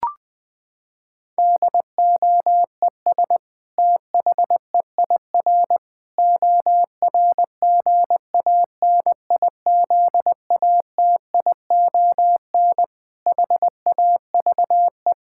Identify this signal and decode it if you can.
Morse code, heard as a keyed tone, about 21 wpm, 700 Hz. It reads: DOES THEIR ORGANIZATION HAVE
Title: Sentences from Top 1000 Words - Review Part 5 - 20wpm